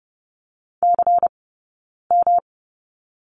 Morse code, heard as a keyed tone, about 30 wpm, spaced as for 9 wpm, 690 Hz. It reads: CM
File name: Lesson 18d - Letter m - Two Characters 30wpm